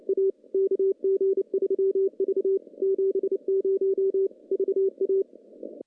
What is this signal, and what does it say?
Morse code, heard as a keyed tone, about 32 wpm, 390 Hz. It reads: A KG3V 70 VA
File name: contests / CW-SS-k5tr- -0100